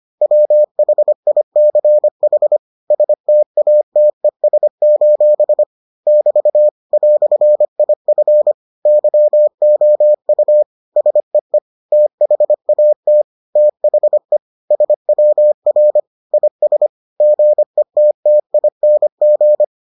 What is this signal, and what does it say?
Morse code, heard as a keyed tone, about 25 wpm, 600 Hz. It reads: WHICH STATES: <BT> "IF YOU SEE THAT THE SWR IS GETTING